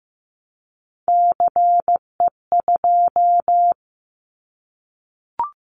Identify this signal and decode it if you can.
Morse code, heard as a keyed tone, about 15 wpm, 700 Hz. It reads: CE2
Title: Callsign Prefixes for Top 100 DXCC - 15wpm